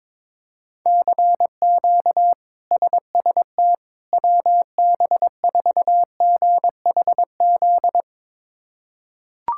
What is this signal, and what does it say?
Morse code, heard as a keyed tone, about 22 wpm, 700 Hz. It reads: CQ SST WB4GHZ
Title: K1USN SST QSO Practice - 22wpm